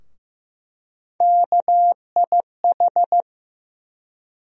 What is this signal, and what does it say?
Morse code, heard as a keyed tone, about 15 wpm, 700 Hz. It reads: KIH
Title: USA Call Sign Postfixes - 3 Letters 15wpm